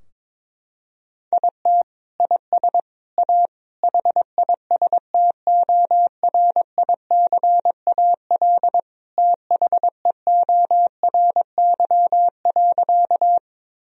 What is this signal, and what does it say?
Morse code, heard as a keyed tone, about 22 wpm, 700 Hz. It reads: IT IS A HISTORICAL THEORY.
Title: Sentences from Top 1000 Words - Review Part 4 - 22wpm